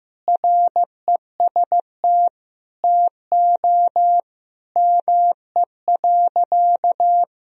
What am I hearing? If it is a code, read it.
Morse code, heard as a keyed tone, about 15 wpm, 700 Hz. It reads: REST TO ME.
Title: Sentences from Top 500 Words and Abbreviations 15wpm